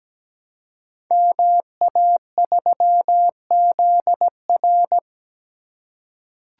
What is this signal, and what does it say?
Morse code, heard as a keyed tone, about 17 wpm, 690 Hz. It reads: MA3ZR
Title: Code Groups - 5 Letter:Numbers - 17wpm